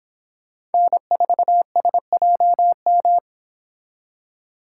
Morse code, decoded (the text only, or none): N4SJM